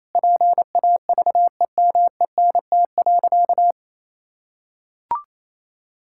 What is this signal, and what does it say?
Morse code, heard as a keyed tone, about 28 wpm, 700 Hz. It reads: PAVEMENT.
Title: Pangram Challenge v2 - 28wpm